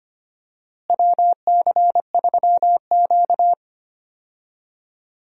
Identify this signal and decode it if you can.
Morse code, heard as a keyed tone, about 25 wpm, 700 Hz. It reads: WC3Q
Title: USA Call Signs - 4 Characters - Repeated 2x - 25wpm